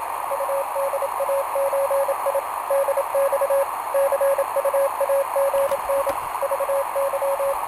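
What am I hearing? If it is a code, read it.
Morse code (Russian alphabet), heard as a keyed tone, about 27 wpm, 550 Hz, with radio noise. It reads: УДАЧИ ДЬ ЦУАГН ЖЫ
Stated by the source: Portable QRP contact on a home-made 3 watt transmitter with a long dipole, received by a station running home-made 1.2 watts, with key clicks and hard keying in question.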